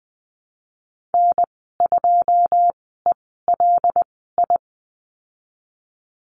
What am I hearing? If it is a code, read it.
Morse code, heard as a keyed tone, about 20 wpm, spaced as for 15 wpm, 700 Hz. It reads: N2ELI
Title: 088 Lesson 15g - Number 5 - Call Signs 20wpm